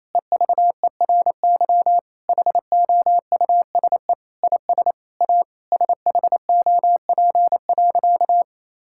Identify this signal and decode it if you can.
Morse code, heard as a keyed tone, about 27 wpm, 700 Hz. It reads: EVERY HOUSE IS A SHOP.
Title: Sentences from Top 1600 Words 28wpm